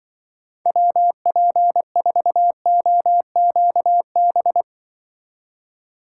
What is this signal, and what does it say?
Morse code, heard as a keyed tone, about 25 wpm, 700 Hz. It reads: WP4OQB
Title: Speed Racing - USA 2x3 Call Signs 20wpm